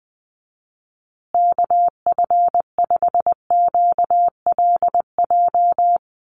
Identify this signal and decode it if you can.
Morse code, heard as a keyed tone, about 20 wpm, 700 Hz. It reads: KF5QLJ
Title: USA Call Signs Repeated 3x 20wpm